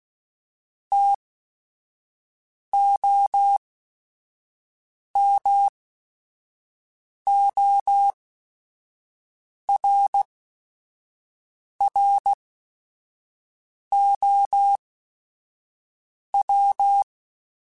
Morse code, heard as a keyed tone, about 16 wpm, spaced as for 5 wpm, 800 Hz. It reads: TOMORROW
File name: Chuckles / Chuckles-mp3s / SoftWords @5